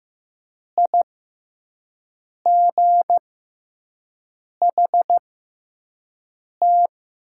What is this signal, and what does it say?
Morse code, heard as a keyed tone, about 15 wpm, spaced as for 5 wpm, 700 Hz. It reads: IGHT
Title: 6x Character Spacing - Top 300 Words - 15wpm